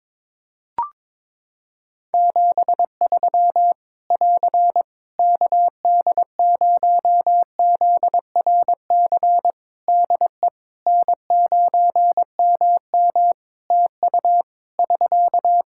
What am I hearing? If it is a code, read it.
Morse code, heard as a keyed tone, about 22 wpm, 700 Hz. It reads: 73 <AR> KD0ZRC DE N9MM TU <SK>
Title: Practice QSO - Exchange 4 - Ending - 22wpm